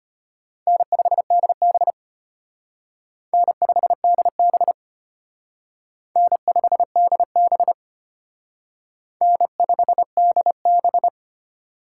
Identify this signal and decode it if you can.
Morse code, heard as a keyed tone, about 32 wpm, 700 Hz. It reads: N5DB N5DB N5DB N5DB